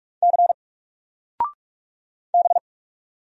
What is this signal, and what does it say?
Morse code, heard as a keyed tone, about 44 wpm, 700 Hz. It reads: C B
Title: Single Letter-Number 45wpm